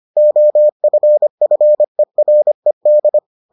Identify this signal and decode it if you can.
Morse code, heard as a keyed tone, about 25 wpm, 600 Hz. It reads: OFFERED